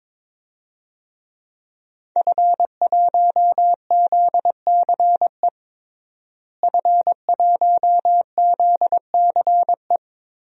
Morse code, heard as a keyed tone, about 21 wpm, 700 Hz. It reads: F1ZCE F1ZCE
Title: DXCC-100 Call Signs - 5 Characters - Repeated 2x - 22wpm